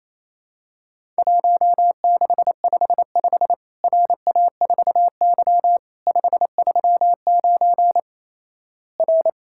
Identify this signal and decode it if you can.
Morse code, heard as a keyed tone, about 27 wpm, 700 Hz. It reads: 1655 RA4Y 539 R
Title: Worked All Europe DX Contest - QTC Exchange - 28wpm